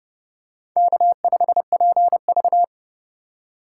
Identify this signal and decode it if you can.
Morse code, heard as a keyed tone, about 27 wpm, 700 Hz. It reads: K5PV